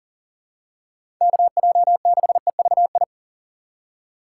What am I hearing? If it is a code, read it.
Morse code, heard as a keyed tone, about 40 wpm, 700 Hz. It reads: KJ6EVI
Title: USA Call Signs - 6 Characters - 2x Character Spacing - 40wpm